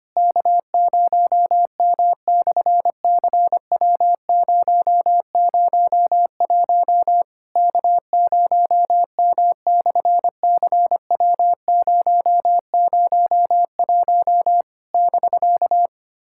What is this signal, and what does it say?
Morse code, heard as a keyed tone, about 24 wpm, 700 Hz. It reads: K0M/CW001 K0M/CW001 <BK>
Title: SOTA QSO Practice 25wpm